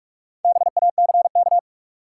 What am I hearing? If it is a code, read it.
Morse code, heard as a keyed tone, about 47 wpm, 700 Hz. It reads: BACK
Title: Sets of 2 Words - New Word Drill - 45wpm